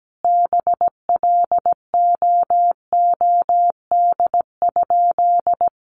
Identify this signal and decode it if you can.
Morse code, heard as a keyed tone, about 17 wpm, 700 Hz. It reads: BLOOD?
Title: Sentences from Top 1400 Words 17wpm